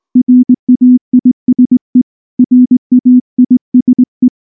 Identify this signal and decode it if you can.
Morse code, heard as a keyed tone, about 19 wpm, 250 Hz. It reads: RAISE RAISE
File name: morse / morse-training-20WPM-10